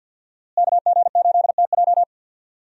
Morse code, heard as a keyed tone, about 50 wpm, 700 Hz. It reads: KG8TJ
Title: USA Call Signs - 5 Characters - Repeated 2x - 50wpm